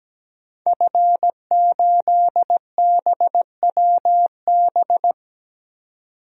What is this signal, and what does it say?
Morse code, heard as a keyed tone, about 17 wpm, 700 Hz. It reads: F8BWB
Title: DXCC-100 Call Signs - 5 Characters - 17wpm